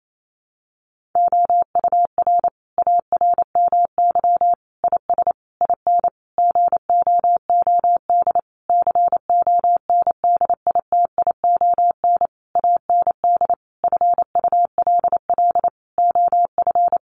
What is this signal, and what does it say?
Morse code, heard as a keyed tone, about 28 wpm, 700 Hz. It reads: OUR ARMY IS IN GOOD CONDITION AND FULL OF
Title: Sentences from Top 1500 Words 28wpm